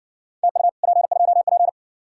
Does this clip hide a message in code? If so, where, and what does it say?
Yes: Morse code from 0.4 to 1.7 s, heard as a keyed tone, about 60 wpm, 700 Hz. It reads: TU P2P